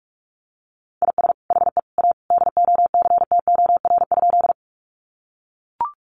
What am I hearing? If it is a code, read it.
Morse code, heard as a keyed tone, about 44 wpm, 700 Hz. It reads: IS HE A DOCTOR?
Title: Sentences from Top 1500 Words 45wpm